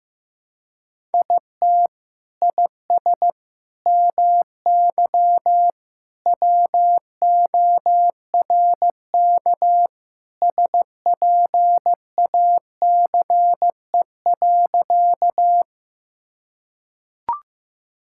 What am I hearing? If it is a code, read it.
Morse code, heard as a keyed tone, about 15 wpm, 700 Hz. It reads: IT IS MY WORK SPACE.
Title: Sentences from Top 1000 Words - Review Part 5 - 15wpm